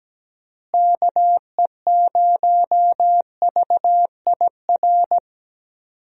Morse code, heard as a keyed tone, about 17 wpm, 700 Hz. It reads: KE0VIR
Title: USA Call Signs - 6 Characters - Repeated 3x - 17wpm